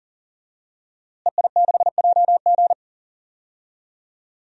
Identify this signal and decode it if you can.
Morse code, heard as a keyed tone, about 40 wpm, 700 Hz. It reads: EI6JG
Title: DXCC-100 Call Signs - 5 Characters - Repeated 3x - 40wpm